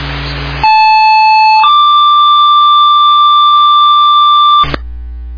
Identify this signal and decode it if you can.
Two-tone page, heard as a one-second tone, then a three-second tone, about 870 then 1180 Hz, with noise. It is heard from 0.6 to 4.6 s.